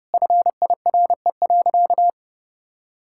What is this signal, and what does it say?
Morse code, heard as a keyed tone, about 30 wpm, 700 Hz. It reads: FIRE.
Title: Sentences from Top 1600 Words 30wpm